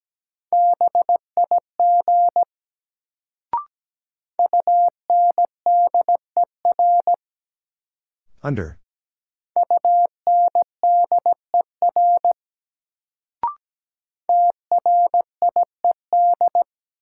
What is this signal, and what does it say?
Morse code, heard as a keyed tone, about 17 wpm, 700 Hz. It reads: BIG UNDER UNDER TRIED
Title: Sets of 2 Words - New Word Drill - 17wpm